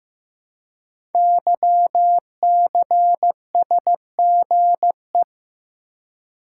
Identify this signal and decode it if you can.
Morse code, heard as a keyed tone, about 15 wpm, 700 Hz. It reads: YCSGE
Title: Code Groups - 5 Letters - 15wpm